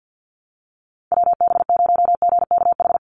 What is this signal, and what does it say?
Morse code, heard as a keyed tone, about 49 wpm, 700 Hz. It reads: WB0GKH
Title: Call Signs Repeated 3x - 50wpm